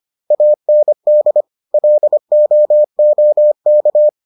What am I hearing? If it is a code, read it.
Morse code, heard as a keyed tone, about 25 wpm, 600 Hz. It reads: AND LOOK